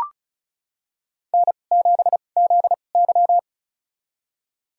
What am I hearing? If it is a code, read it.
Morse code, heard as a keyed tone, about 35 wpm, spaced as for 25 wpm, 700 Hz. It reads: N7ZY
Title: USA Call Signs - 4 Characters - 2x Character Spacing - 35wpm